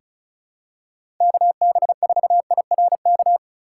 Morse code, heard as a keyed tone, about 36 wpm, 700 Hz. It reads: KB4IRK
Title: USA Call Signs - 6 Characters - 4x Character Spacing - 35wpm